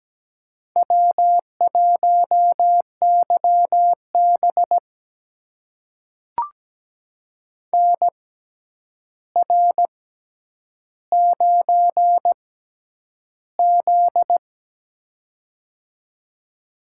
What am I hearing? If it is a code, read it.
Morse code, heard as a keyed tone, about 17 wpm, spaced as for 9 wpm, 700 Hz. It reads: W1YB NR9Z
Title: USA Call Signs - 4 Characters - 6x Character Spacing - 17wpm